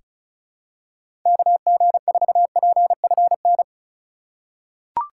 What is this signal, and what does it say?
Morse code, heard as a keyed tone, about 36 wpm, 700 Hz. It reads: KG4PFN
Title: USA Call Signs - 6 Characters - 8x Character Spacing - 35wpm